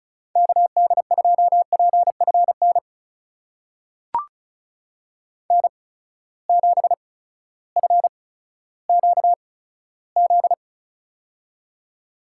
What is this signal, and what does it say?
Morse code, heard as a keyed tone, about 36 wpm, spaced as for 16 wpm, 700 Hz. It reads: KD2PFN N7FQZ
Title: USA Call Signs - 8x Character Spacing - 35wpm